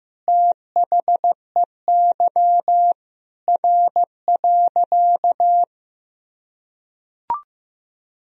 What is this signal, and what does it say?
Morse code, heard as a keyed tone, about 15 wpm, 700 Hz. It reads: THEY R.
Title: Sentences from Top 100 Words and Abbreviations 15wpm